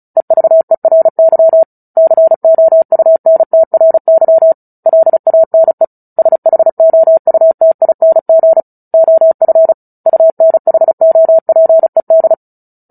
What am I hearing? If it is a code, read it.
Morse code, heard as a keyed tone, about 34 wpm, 650 Hz. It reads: EVERY COUNTRY LANE SHOUTING OF UNHOPED
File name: Book Two / 09-Chapter 9 - 35 WPM